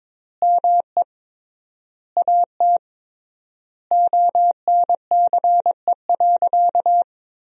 Morse code, heard as a keyed tone, about 22 wpm, 700 Hz. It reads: ME AT ONCE.
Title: 3x Word-Spacing - Sentences from Top 500 Words - 22wpm